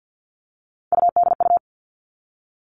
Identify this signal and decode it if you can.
Morse code, heard as a keyed tone, about 50 wpm, 700 Hz. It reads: UDU
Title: Three Challenging Letter-Number Combinations 50wpm